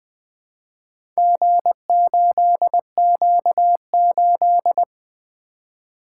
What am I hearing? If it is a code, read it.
Morse code, heard as a keyed tone, about 20 wpm, 700 Hz. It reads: G8Q8
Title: Four Letter-Number Combinations - 20wpm